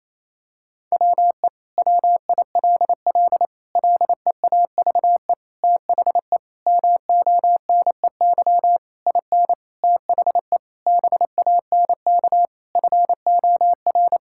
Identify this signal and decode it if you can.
Morse code, heard as a keyed tone, about 28 wpm, 700 Hz. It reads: WE WILL LEAVE THE MONEY IN THE BANK FOR
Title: Sentences from Top 1000 Words - Review Part 6 - 28wpm